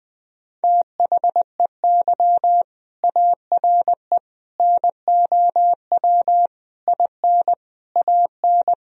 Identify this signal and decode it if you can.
Morse code, heard as a keyed tone, about 20 wpm, 700 Hz. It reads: THEY ARE NOW IN AN
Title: Sentences from Top 1000 Words - Review Part 2 - 20wpm